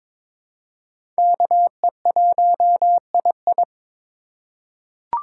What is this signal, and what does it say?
Morse code, heard as a keyed tone, about 21 wpm, 700 Hz. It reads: KE1II
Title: USA Call Signs - 5 Characters - 3x Character Spacing - 22wpm